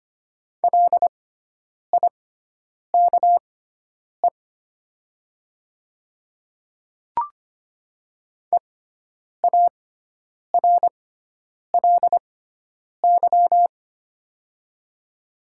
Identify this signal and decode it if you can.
Morse code, heard as a keyed tone, about 25 wpm, spaced as for 9 wpm, 700 Hz. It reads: LIKE EARLY
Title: 6x Character Spacing - Top 500 Words - 25wpm